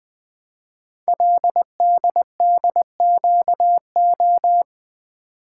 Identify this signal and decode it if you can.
Morse code, heard as a keyed tone, about 20 wpm, 700 Hz. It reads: LDDQO
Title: Code Groups - 5 Letters - 20wpm